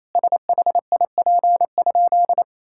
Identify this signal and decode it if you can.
Morse code, heard as a keyed tone, about 28 wpm, 700 Hz. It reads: SHIP?
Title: Sentences from Top 1000 Words - Review Part 6 - 28wpm